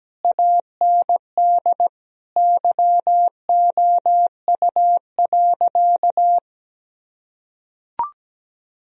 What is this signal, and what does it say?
Morse code, heard as a keyed tone, about 17 wpm, 700 Hz. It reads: AND YOU.
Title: Sentences from Top 1000 Words - Review Part 1 - 17wpm